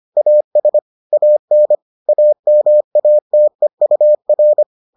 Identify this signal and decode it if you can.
Morse code, heard as a keyed tone, about 25 wpm, 600 Hz. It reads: AS AN AMATEUR